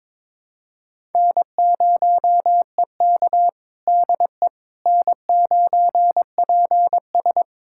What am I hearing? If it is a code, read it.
Morse code, heard as a keyed tone, about 22 wpm, 700 Hz. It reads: N0EK DE N9PS
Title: USA Callsign DE Callsign - 4 Characters - 22wpm